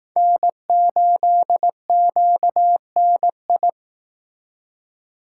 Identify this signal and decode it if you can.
Morse code, heard as a keyed tone, about 18 wpm, 700 Hz. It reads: N8QNI
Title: Speed Racing - USA 1x3 Call Signs 15wpm